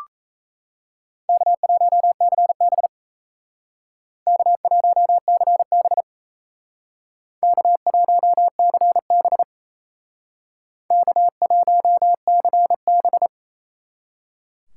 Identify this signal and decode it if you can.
Morse code, heard as a keyed tone, about 37 wpm, 700 Hz. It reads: K1CB K1CB K1CB K1CB